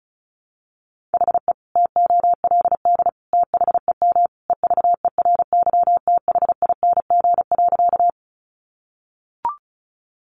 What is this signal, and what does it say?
Morse code, heard as a keyed tone, about 34 wpm, 700 Hz. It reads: HE TOLD THEM EVERYTHING.
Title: Sentences from Top 700 Words 35wpm